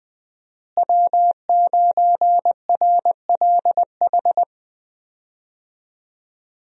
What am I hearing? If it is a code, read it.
Morse code, heard as a keyed tone, about 20 wpm, 700 Hz. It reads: W9RLH
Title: USA Call Signs - 5 Characters - Repeated 2x - 20wpm